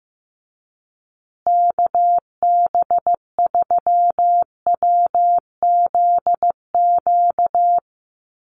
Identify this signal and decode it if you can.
Morse code, heard as a keyed tone, about 15 wpm, 700 Hz. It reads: KB3WZQ
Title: USA Call Signs Repeated 3x 15wpm